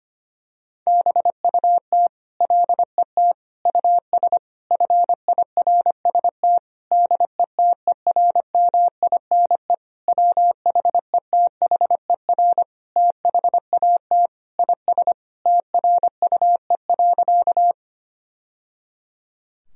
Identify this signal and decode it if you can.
Morse code, heard as a keyed tone, about 25 wpm, 700 Hz. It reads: BUT LET US FIRST DETERMINE WHETHER THAT IS TRUE.
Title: Sentences from Top 1000 Words - Review Part 4 - 25wpm